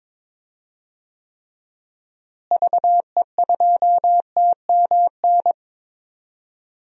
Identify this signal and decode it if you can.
Morse code, heard as a keyed tone, about 22 wpm, 700 Hz. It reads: VE2TMN
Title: DXCC-100 Call Signs - 6 Characters - Repeated 2x - 22wpm